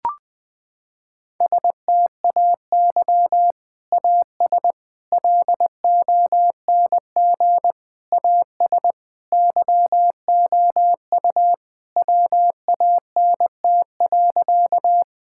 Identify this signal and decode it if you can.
Morse code, heard as a keyed tone, about 20 wpm, 700 Hz. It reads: STAY AS LONG AS YOU WANT.
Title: Sentences with 3-Word Phrases - 20wpm